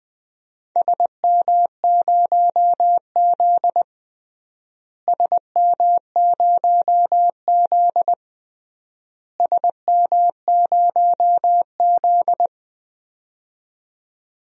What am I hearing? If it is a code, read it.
Morse code, heard as a keyed tone, about 20 wpm, 700 Hz. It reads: SM0Z SM0Z SM0Z
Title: DXCC-100 Call Signs - 4 Characters - Repeated 3x - 20wpm